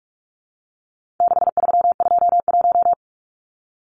Morse code, heard as a keyed tone, about 44 wpm, 700 Hz. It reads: 6321
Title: Numbers - 4-Digit 45wpm